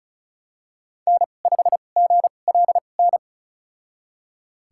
Morse code, heard as a keyed tone, about 36 wpm, spaced as for 25 wpm, 700 Hz. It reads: N5GLN
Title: USA Call Signs - 5 Characters - 2x Character Spacing - 35wpm